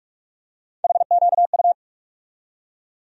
Morse code, heard as a keyed tone, about 46 wpm, 700 Hz. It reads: HQU
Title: USA Call Sign Postfixes - 3 Letters 45wpm